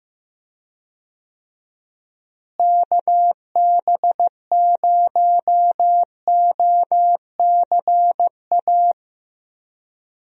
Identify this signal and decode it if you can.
Morse code, heard as a keyed tone, about 15 wpm, 700 Hz. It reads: KB0OCA
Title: USA Call Signs - 6 Characters - Repeated 2x - 15wpm